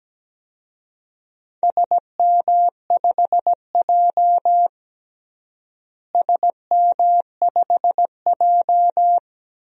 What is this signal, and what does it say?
Morse code, heard as a keyed tone, about 17 wpm, 700 Hz. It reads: SM5J SM5J
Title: DXCC-100 Call Signs - 4 Characters - Repeated 3x - 17wpm